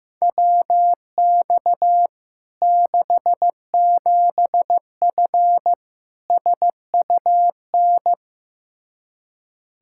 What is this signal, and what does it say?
Morse code, heard as a keyed tone, about 16 wpm, 700 Hz. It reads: WX 67F SUN
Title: K1USN SST QSO Practice 15wpm